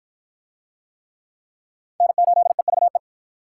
Morse code, heard as a keyed tone, about 49 wpm, 700 Hz. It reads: N8EVE